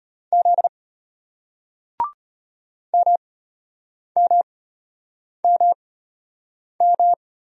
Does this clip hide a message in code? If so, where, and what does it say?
Yes: Morse code from 0.3 to 7.1 s, heard as a keyed tone, about 33 wpm, spaced as for 8 wpm, 700 Hz. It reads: Z MMMM